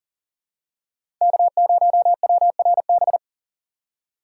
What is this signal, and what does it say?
Morse code, heard as a keyed tone, about 40 wpm, 700 Hz. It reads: K0WRB
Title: USA Call Signs - 5 Characters - 6x Character Spacing - 40wpm